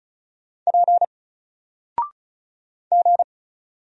Morse code, heard as a keyed tone, about 35 wpm, 700 Hz. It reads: P G
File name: Single-Letter-Number-35wpm